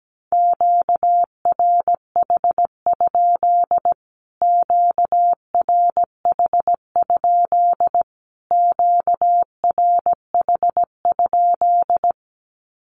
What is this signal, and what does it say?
Morse code, heard as a keyed tone, about 17 wpm, 700 Hz. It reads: QRH? QRH? QRH?
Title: Q-Codes Repeated 3x 17wpm